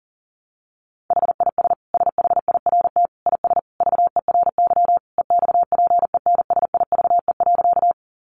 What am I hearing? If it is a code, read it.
Morse code, heard as a keyed tone, about 40 wpm, 700 Hz. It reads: HIS SHIRT IS VERY EXPENSIVE.
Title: Additional Easy Phrases 40wpm